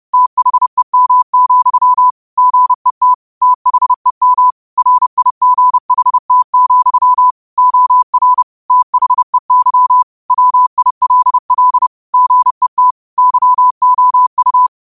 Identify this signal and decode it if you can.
Morse code, heard as a keyed tone, about 30 wpm, 1000 Hz. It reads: THEM, GET THEM RIGHT, OR THEY WILL GET YOU
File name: qotdfiles / 30wpm